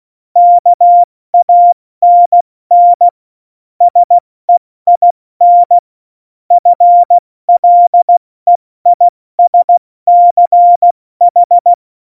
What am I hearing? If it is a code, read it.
Morse code, heard as a keyed tone, about 16 wpm, 700 Hz. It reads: KANN SEIN FLEISCH
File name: Morsekurs / Witze / Witz 26. November 2019, 70 F80